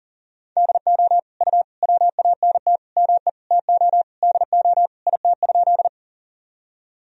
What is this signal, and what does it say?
Morse code, heard as a keyed tone, about 40 wpm, 700 Hz. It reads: DO U WANT ME TO DO IT?
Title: Sentences from Top 200 Words and Abbreviations 40wpm